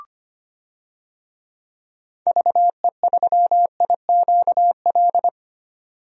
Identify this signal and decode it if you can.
Morse code, heard as a keyed tone, about 25 wpm, 700 Hz. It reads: VE3IQL
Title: DXCC-100 Call Signs - 6 Characters - 25wpm